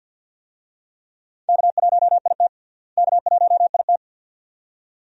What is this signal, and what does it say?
Morse code, heard as a keyed tone, about 49 wpm, 700 Hz. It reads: K1IT K1IT
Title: USA Call Signs - 4 Characters - Repeated 2x - 50wpm